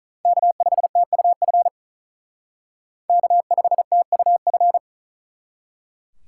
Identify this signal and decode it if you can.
Morse code, heard as a keyed tone, about 39 wpm, 700 Hz. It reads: K5TUF K5TUF
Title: Speed Racing - USA 1x3 Call Signs 35wpm